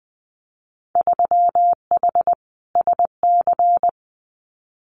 Morse code, heard as a keyed tone, about 20 wpm, 700 Hz. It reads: 3H SC
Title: Winter Field Day Exchanges 20wpm